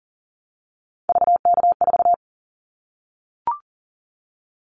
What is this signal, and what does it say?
Morse code, heard as a keyed tone, about 38 wpm, 700 Hz. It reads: VK4
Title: Callsign Prefixes for Top 50 DXCC - 40wpm